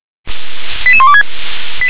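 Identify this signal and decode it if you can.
Five-tone selective call (ZVEI), heard as quick runs of five tones, tones of about 70 ms. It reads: 88126
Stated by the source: ZVEI five-tone call received by SDR.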